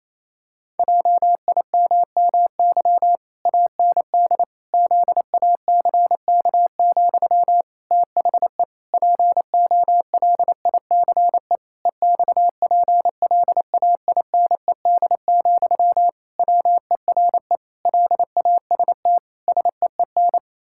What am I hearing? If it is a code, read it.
Morse code, heard as a keyed tone, about 27 wpm, 700 Hz. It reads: JIMMY AND ZACK, THE POLICE EXPLAINED, WERE LAST SEEN
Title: Pangram Challenge v2 - 28wpm